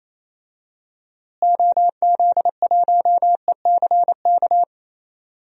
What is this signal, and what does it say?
Morse code, heard as a keyed tone, about 28 wpm, 700 Hz. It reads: OZ1ECK
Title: DXCC-100 Call Signs - 6 Characters - Repeated 3x - 28wpm